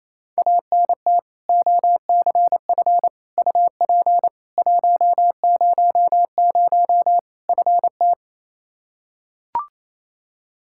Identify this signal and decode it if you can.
Morse code, heard as a keyed tone, about 27 wpm, 700 Hz. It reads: ANT OCF UP 100 FT
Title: K1USN SST QSO Practice 28wpm